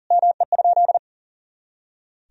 Morse code, heard as a keyed tone, about 37 wpm, 700 Hz. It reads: ME?
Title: Sentences from Top 200 Words and Abbreviations 40wpm